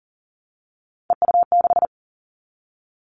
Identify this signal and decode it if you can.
Morse code, heard as a keyed tone, about 40 wpm, 700 Hz. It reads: EU6
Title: Callsign Prefixes for Top 100 DXCC - 40wpm